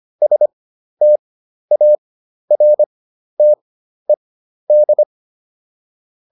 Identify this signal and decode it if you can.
Morse code, heard as a keyed tone, about 25 wpm, spaced as for 12 wpm, 600 Hz. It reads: STARTED